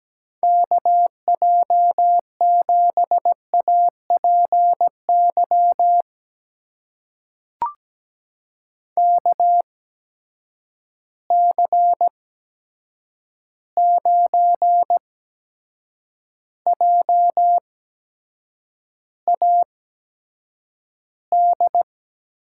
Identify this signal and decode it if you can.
Morse code, heard as a keyed tone, about 17 wpm, spaced as for 7 wpm, 700 Hz. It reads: KJ7APY KC9JAD
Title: USA Call Signs - 6 Characters - 8x Character Spacing - 17wpm